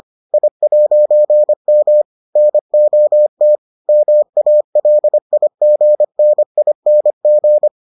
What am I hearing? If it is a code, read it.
Morse code, heard as a keyed tone, about 25 wpm, 600 Hz. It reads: I'M NOT MALIGNING